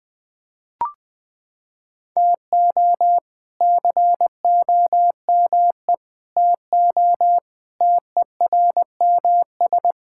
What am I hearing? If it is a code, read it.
Morse code, heard as a keyed tone, about 20 wpm, 700 Hz. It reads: TO COME TO TERMS
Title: Sets of 8 Words - 20wpm